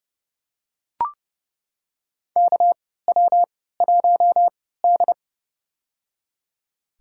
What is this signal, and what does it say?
Morse code, heard as a keyed tone, about 30 wpm, spaced as for 17 wpm, 700 Hz. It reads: KW1D
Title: USA Call Signs - 4 Characters - 3x Character Spacing - 30wpm